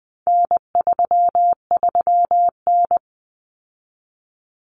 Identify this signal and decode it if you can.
Morse code, heard as a keyed tone, about 20 wpm, 700 Hz. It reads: N33N
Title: Common Cut Numbers - 4-Digit 20wpm